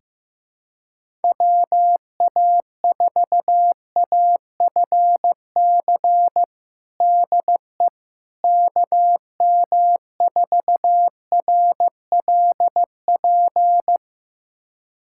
Morse code, heard as a keyed tone, about 15 wpm, 700 Hz. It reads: WA4AFC DE KM4RLP
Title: USA Callsign DE Callsign - 6 Characters - 15wpm